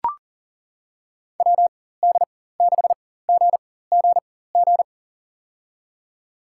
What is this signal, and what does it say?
Morse code, heard as a keyed tone, about 37 wpm, spaced as for 18 wpm, 700 Hz. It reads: WD6GGG
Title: USA Call Signs - 4x Character Spacing - 40wpm